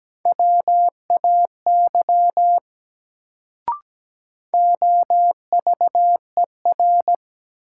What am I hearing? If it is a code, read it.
Morse code, heard as a keyed tone, about 17 wpm, 700 Hz. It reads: WAY OVER
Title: Sets of 2 Words - New Word Drill - 17wpm